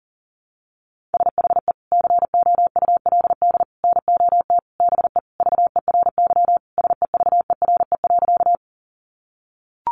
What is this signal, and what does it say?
Morse code, heard as a keyed tone, about 40 wpm, 700 Hz. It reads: SHE COULD NOT BE VERY SEVERE.
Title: Sentences from Top 1400 Words 40wpm